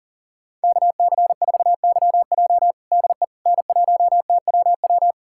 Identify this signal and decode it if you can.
Morse code, heard as a keyed tone, about 40 wpm, 700 Hz. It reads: KC4YJ DE N1TWW
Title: USA Callsign DE Callsign - 5 Characters - 40wpm